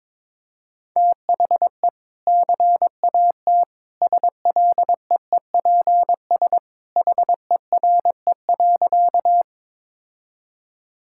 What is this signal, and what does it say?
Morse code, heard as a keyed tone, about 22 wpm, 700 Hz. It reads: THE CAT SLEEPS HERE.
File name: practice / 22wpm / Easy Sentences with Hints - 22wpm